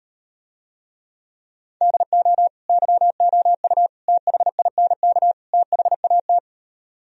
Morse code, heard as a keyed tone, about 38 wpm, 700 Hz. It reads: DO YOU THINK THAT